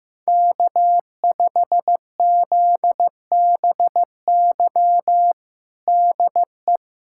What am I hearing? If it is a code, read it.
Morse code, heard as a keyed tone, about 15 wpm, 700 Hz. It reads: K5ZBY DE
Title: USA Callsign DE Callsign 15wpm